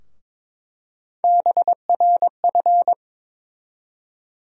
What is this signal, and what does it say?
Morse code, heard as a keyed tone, about 22 wpm, 700 Hz. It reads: BRF
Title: USA Call Sign Postfixes - 3 Letters 22wpm